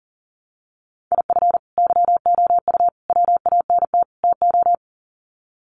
Morse code, heard as a keyed tone, about 40 wpm, 700 Hz. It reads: IF YOU WANT TO
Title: Sets of 4 Words 40wpm